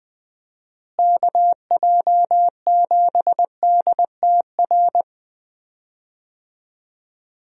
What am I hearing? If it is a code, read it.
Morse code, heard as a keyed tone, about 21 wpm, 700 Hz. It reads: KJ7DTR
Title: USA Call Signs - 6 Characters - Repeated 3x - 20wpm